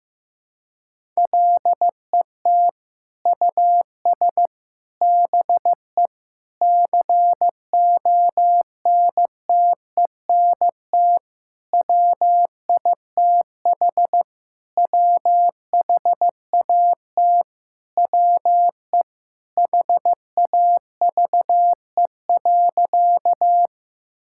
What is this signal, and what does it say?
Morse code, heard as a keyed tone, about 15 wpm, 700 Hz. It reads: LET US BE CONTENT WITH WHAT WE HAVE.
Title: Sentences from Top 1000 Words - Review Part 6 - 15wpm